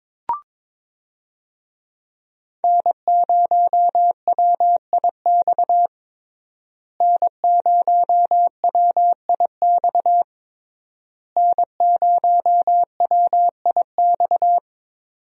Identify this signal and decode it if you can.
Morse code, heard as a keyed tone, about 22 wpm, 700 Hz. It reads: N0WIX N0WIX N0WIX
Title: USA Call Signs - 5 Characters - Repeated 3x - 22wpm